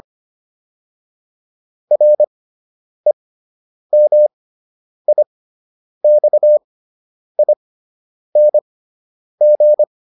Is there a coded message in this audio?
Yes: Morse code from 1.9 to 9.8 s, heard as a keyed tone, about 25 wpm, spaced as for 9 wpm, 600 Hz. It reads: REMIXING